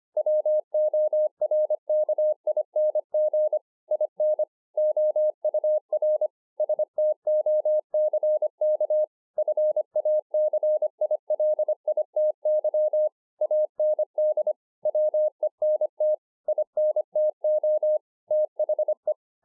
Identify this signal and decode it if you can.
Morse code, heard as a keyed tone, about 25 wpm, 610 Hz. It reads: WORKING IN OUR STOCK FACILITY AND WENT INTO THE